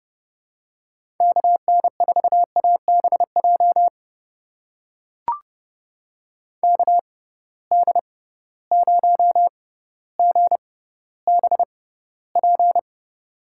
Morse code, heard as a keyed tone, about 30 wpm, spaced as for 15 wpm, 700 Hz. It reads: KN4ABJ KD0GBP
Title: USA Call Signs - 6 Characters - 6x Character Spacing - 30wpm